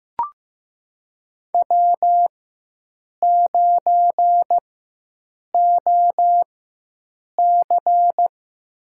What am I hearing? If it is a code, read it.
Morse code, heard as a keyed tone, about 15 wpm, spaced as for 7 wpm, 700 Hz. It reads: W9OC
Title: USA Call Signs - 4 Characters - 4x Character Spacing - 15wpm